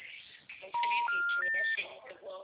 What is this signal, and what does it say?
Special information tone, with speech.